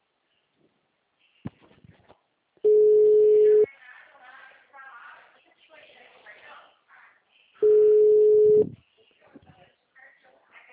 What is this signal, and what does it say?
Ringback tone (Europe, 425 Hz), with speech.
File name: out- -2010- - - .144